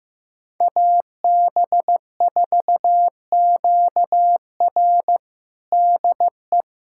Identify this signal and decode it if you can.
Morse code, heard as a keyed tone, about 16 wpm, 700 Hz. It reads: AB4QR DE
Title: USA Callsign DE Callsign - 5 Characters - 15wpm